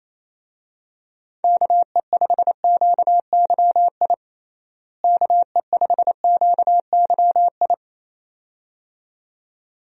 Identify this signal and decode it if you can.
Morse code, heard as a keyed tone, about 28 wpm, 700 Hz. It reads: KE5QYI KE5QYI
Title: USA Call Signs - 6 Characters - Repeated 2x - 28wpm